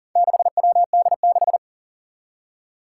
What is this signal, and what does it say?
Morse code, heard as a keyed tone, about 43 wpm, 700 Hz. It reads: 6WD6
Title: Four Challenging Letter-Number Combinations 40wpm